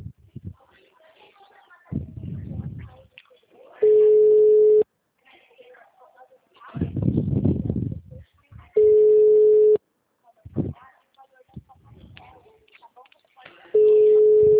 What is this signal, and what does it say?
Ringback tone (Europe, 425 Hz), with speech.
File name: out- -2011- - - .1205